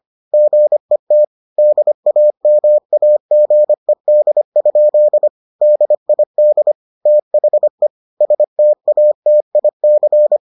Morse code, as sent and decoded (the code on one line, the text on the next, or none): --. - / -.. .- -- .- --. -.. ..--.. / -.. .. -.. / - .... / ... - .- - .. -.-.
GET DAMAGED? DID THE STATIC